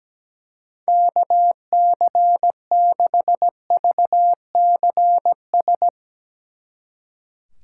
Morse code, heard as a keyed tone, about 17 wpm, 700 Hz. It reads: KC6VCS
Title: Speed Racing - USA Call Signs 17wpm